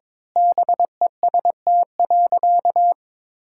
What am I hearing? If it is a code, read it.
Morse code, heard as a keyed tone, about 22 wpm, 700 Hz. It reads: BEST.
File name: practice / 22wpm / Sentences with 2-Word Phrases - 22wpm